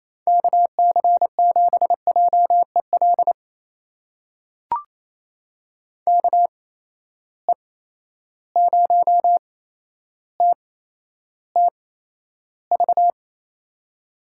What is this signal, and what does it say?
Morse code, heard as a keyed tone, about 28 wpm, spaced as for 12 wpm, 700 Hz. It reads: KC7JEL KE0TTV
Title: USA Call Signs - 6 Characters - 8x Character Spacing - 28wpm